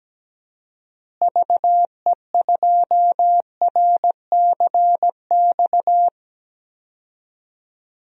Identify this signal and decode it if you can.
Morse code, heard as a keyed tone, about 17 wpm, 700 Hz. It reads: VE2RCX
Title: DXCC-100 Call Signs - 6 Characters - Repeated 3x - 17wpm